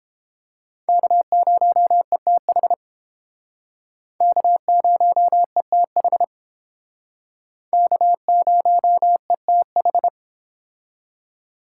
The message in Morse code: K0ETH K0ETH K0ETH

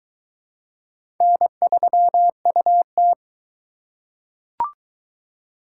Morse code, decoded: N3UT

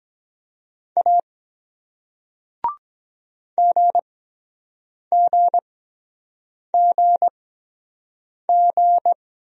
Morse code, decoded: A GGGG

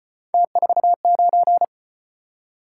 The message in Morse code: T49

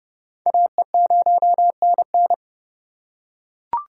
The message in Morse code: AE0NN

